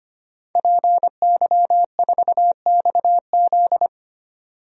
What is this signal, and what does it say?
Morse code, heard as a keyed tone, about 25 wpm, 700 Hz. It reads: PY4XZ
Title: DXCC-100 Call Signs - 5 Characters - 25wpm